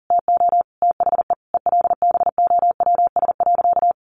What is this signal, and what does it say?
Morse code, heard as a keyed tone, about 40 wpm, 700 Hz. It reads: TO THE ELBOWS.
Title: A Sound Of Thunder 40wpm